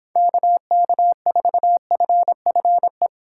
Morse code, heard as a keyed tone, about 27 wpm, 700 Hz. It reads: KK4FFE
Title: Speed Racing - USA 2x3 Call Signs 22wpm